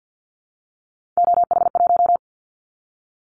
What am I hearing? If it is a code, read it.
Morse code, heard as a keyed tone, about 50 wpm, 700 Hz. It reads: OH1
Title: Callsign Prefixes for Top 50 DXCC - 50wpm